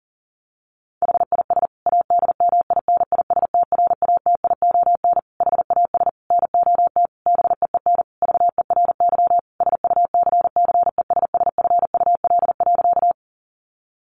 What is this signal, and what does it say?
Morse code, heard as a keyed tone, about 40 wpm, 700 Hz. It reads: HIS ADMINISTRATION HAS NOT BEEN VERY SUCCESSFUL.
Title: Sentences from Top 1300 Words 40wpm